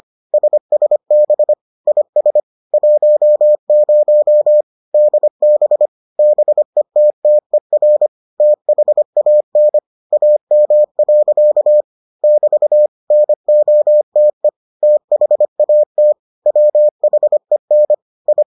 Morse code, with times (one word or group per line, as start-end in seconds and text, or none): 0.3-1.5 SSB
1.9-2.4 IS
2.7-4.6 10
4.9-5.9 DB
6.2-8.1 BETTER
8.4-9.8 THAN
10.1-11.8 AM.
12.2-12.9 <BT>
13.1-14.5 NOTE
14.8-16.1 THAT
16.5-17.9 WHEN
18.3-18.4 I